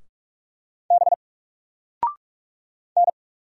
Morse code, heard as a keyed tone, about 46 wpm, 700 Hz. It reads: B N
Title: Single Character (Letter:Number:Punctuation:Prosign) 45wpm